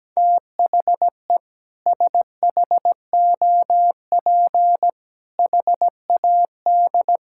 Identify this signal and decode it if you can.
Morse code, heard as a keyed tone, about 17 wpm, 700 Hz. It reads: THE SHOP HAD
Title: Sentences from Top 1600 Words 17wpm